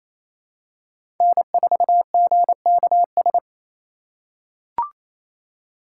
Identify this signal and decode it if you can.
Morse code, heard as a keyed tone, about 28 wpm, 700 Hz. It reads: N4GKS
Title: USA Call Signs - 4x Character Spacing - 28wpm